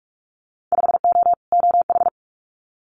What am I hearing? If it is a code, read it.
Morse code, heard as a keyed tone, about 46 wpm, 690 Hz. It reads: 5O OH
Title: Winter Field Day Exchanges Encore 45wpm